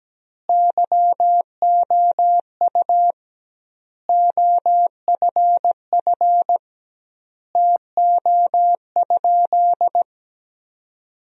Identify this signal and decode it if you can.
Morse code, heard as a keyed tone, about 17 wpm, 700 Hz. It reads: YOU OFF TO?
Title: 2x Word-Spacing - Sentences from Top 200 Words - 17wpm